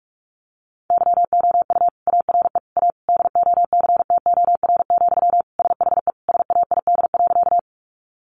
Morse code, heard as a keyed tone, about 44 wpm, 700 Hz. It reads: YOU ARE A DOCTOR, SHE SAID.
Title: Sentences from Top 1500 Words 45wpm